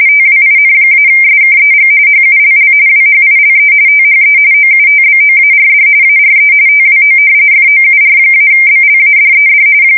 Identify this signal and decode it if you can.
RTTY, about 45.45 baud. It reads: TWIN-SKEG HULL WHILE THE OUTBOARD ARE TWO FOUR-BLADED
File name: fsk170 / battleshipnj